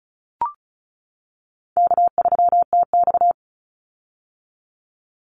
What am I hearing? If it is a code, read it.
Morse code, heard as a keyed tone, about 34 wpm, 700 Hz. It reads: K3TX
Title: Call Signs - 35wpm